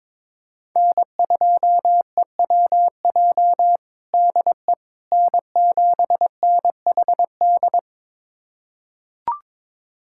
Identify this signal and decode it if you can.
Morse code, heard as a keyed tone, about 22 wpm, 700 Hz. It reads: N2EWJ DE N7NHD
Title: USA Callsign DE Callsign - 5 Characters - 22wpm